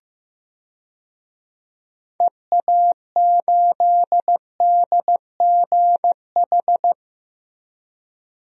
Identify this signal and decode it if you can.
Morse code, heard as a keyed tone, about 15 wpm, 700 Hz. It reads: EA8DGH
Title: DXCC-100 Call Signs - 6 Characters - 15wpm